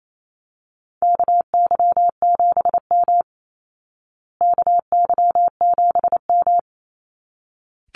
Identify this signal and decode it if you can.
Morse code, heard as a keyed tone, about 28 wpm, 700 Hz. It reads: KY7M KY7M
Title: Call Signs Repeated 3x - 28wpm